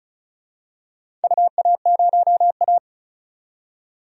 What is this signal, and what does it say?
Morse code, heard as a keyed tone, about 36 wpm, 700 Hz. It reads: UA0A
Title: DXCC-100 Call Signs - 4 Characters - 35wpm